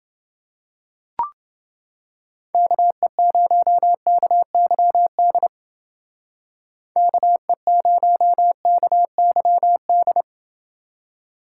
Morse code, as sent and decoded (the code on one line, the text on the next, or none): -.- . ----- -.- -.-- -.. / -.- . ----- -.- -.-- -..
KE0KYD KE0KYD